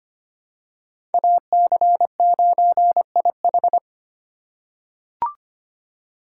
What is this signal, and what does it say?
Morse code, heard as a keyed tone, about 25 wpm, 700 Hz. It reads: AC9IH
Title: USA Call Signs - 5 Characters - 4x Character Spacing - 25wpm